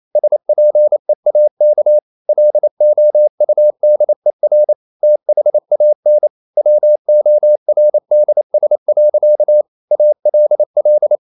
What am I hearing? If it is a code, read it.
Morse code, heard as a keyed tone, about 28 wpm, 600 Hz. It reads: SPEAK LOUDER THAN WORDS. ALL
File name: CW / Practice / EnglishSayings28